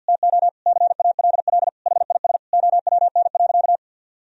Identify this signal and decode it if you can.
Morse code, heard as a keyed tone, about 52 wpm, 700 Hz. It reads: TO CALL HIS OWN.
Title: Sentences from Top 1000 Words - Review Part 2 - 50wpm